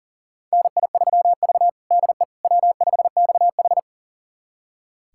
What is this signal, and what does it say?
Morse code, heard as a keyed tone, about 40 wpm, 700 Hz. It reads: NI3V DE W5XH
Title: USA Callsign DE Callsign - 4 Characters - 40wpm